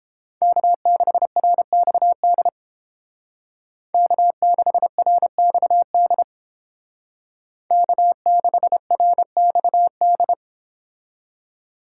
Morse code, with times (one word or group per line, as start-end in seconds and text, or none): 0.4-2.5 K6RXD
3.9-6.2 K6RXD
7.7-10.3 K6RXD